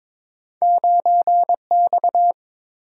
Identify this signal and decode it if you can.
Morse code, heard as a keyed tone, about 30 wpm, 700 Hz. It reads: 9X